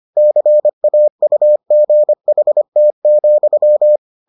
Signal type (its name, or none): Morse code